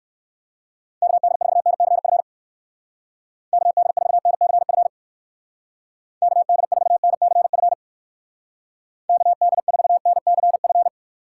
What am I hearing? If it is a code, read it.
Morse code, heard as a keyed tone, about 60 wpm, 700 Hz. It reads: KD4NCF KD4NCF KD4NCF KD4NCF